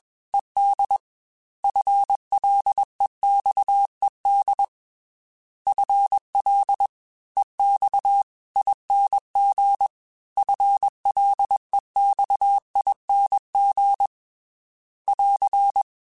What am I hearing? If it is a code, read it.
Morse code, heard as a keyed tone, about 20 wpm, 800 Hz. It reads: ED FLEXED FL EX ING FLEXING <AR>
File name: FL / FL mp3s / Fast FL @20